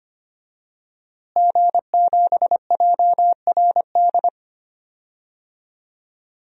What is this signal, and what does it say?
Morse code, heard as a keyed tone, about 25 wpm, 700 Hz. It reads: G7JRD
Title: DXCC-100 Call Signs - 5 Characters - Repeated 3x - 25wpm